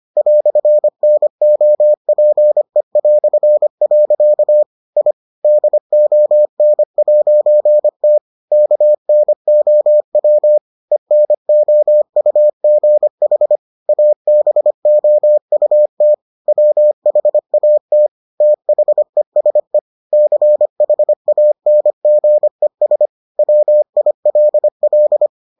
Morse code, heard as a keyed tone, about 25 wpm, 600 Hz. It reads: "NOPE". I DON'T KNOW ENOUGH ABOUT WHAT THESE CHANGES WILL